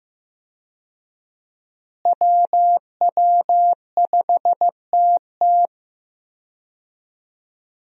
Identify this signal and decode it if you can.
Morse code, heard as a keyed tone, about 15 wpm, 700 Hz. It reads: WW5TT
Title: USA Call Signs - 5 Characters - Repeated 3x - 15wpm